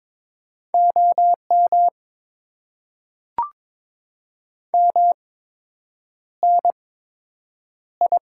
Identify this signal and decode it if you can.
Morse code, heard as a keyed tone, about 22 wpm, spaced as for 8 wpm, 700 Hz. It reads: OM MNI
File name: Top 90 QSO Elements - 8x Character Spacing - 22wpm